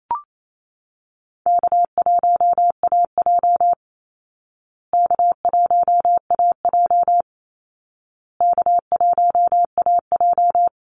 Morse code, heard as a keyed tone, about 28 wpm, 700 Hz. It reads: K1AJ K1AJ K1AJ
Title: Call Signs Repeated 2x - 28wpm